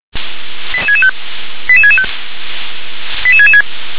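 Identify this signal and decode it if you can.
Five-tone selective call (ZVEI), heard as quick runs of five tones, tones of about 70 ms. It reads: 88665 88665 88666
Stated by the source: ZVEI five-tone call 88852 received by SDR.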